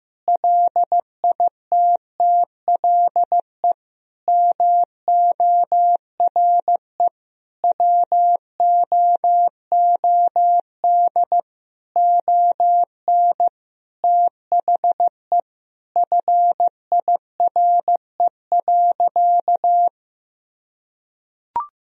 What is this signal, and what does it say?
Morse code, heard as a keyed tone, about 15 wpm, 700 Hz. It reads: LITTLE MORE WOOD ON THE FIRE.
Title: Sentences from Top 1000 Words - Review Part 2 - 15wpm